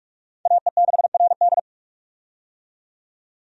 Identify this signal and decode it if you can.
Morse code, heard as a keyed tone, about 46 wpm, 700 Hz. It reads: AE6RD
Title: USA Call Signs - 5 Characters - 45wpm